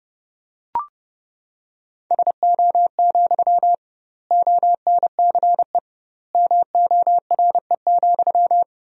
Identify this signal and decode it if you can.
Morse code, heard as a keyed tone, about 30 wpm, 700 Hz. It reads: SO, ONCE MORE,
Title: 2x Word-Spacing - Sentences from Top 300 Words - 30wpm